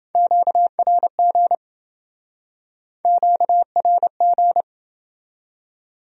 Morse code, heard as a keyed tone, about 29 wpm, 700 Hz. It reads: QRG QRG